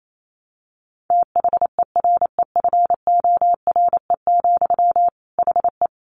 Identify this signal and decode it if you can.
Morse code, heard as a keyed tone, about 28 wpm, 700 Hz. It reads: THEREFORE, HE
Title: Sentences from Top 800 Words 28wpm